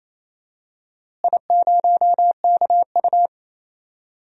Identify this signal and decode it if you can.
Morse code, heard as a keyed tone, about 28 wpm, 700 Hz. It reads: I0KU
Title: DXCC-100 Call Signs - 4 Characters - Repeated 3x - 28wpm